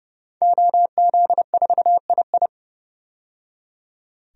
Morse code, heard as a keyed tone, about 32 wpm, 700 Hz. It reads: OZ4II
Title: DXCC-100 Call Signs - 5 Characters - 30wpm